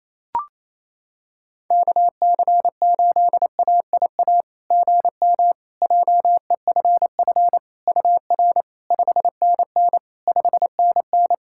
Morse code, heard as a keyed tone, about 28 wpm, 700 Hz. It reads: KC8AIA GM JEFF UR 5NN 5NN
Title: SOTA QSO Practice 28wpm